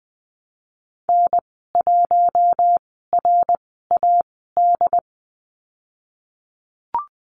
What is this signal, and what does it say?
Morse code, heard as a keyed tone, about 20 wpm, spaced as for 14 wpm, 700 Hz. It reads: N1RAD
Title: 122 Lesson 20g - Number 3 - Call Signs 20wpm